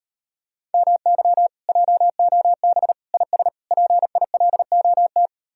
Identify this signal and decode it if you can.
Morse code, heard as a keyed tone, about 38 wpm, 700 Hz. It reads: MY JOB IS PILOT